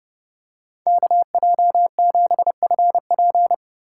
Morse code, heard as a keyed tone, about 30 wpm, 700 Hz. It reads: KJ7FP